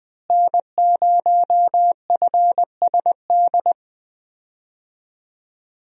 Morse code, heard as a keyed tone, about 20 wpm, 700 Hz. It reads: N0FSD
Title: USA Call Signs - 5 Characters - Repeated 3x - 20wpm